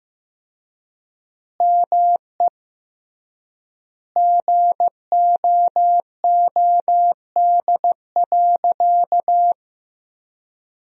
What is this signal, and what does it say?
Morse code, heard as a keyed tone, about 15 wpm, 700 Hz. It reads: ME GOOD.
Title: 3x Word-Spacing - Sentences from Top 500 Words - 15wpm